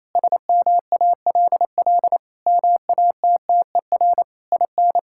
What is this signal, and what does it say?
Morse code, heard as a keyed tone, about 28 wpm, 700 Hz. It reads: SMALL MATTER IN
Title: Sentences from Top 1000 Words - Review Part 5 - 28wpm